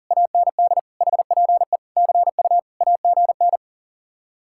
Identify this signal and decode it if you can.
Morse code, heard as a keyed tone, about 53 wpm, 700 Hz. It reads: AND HPE CU AGN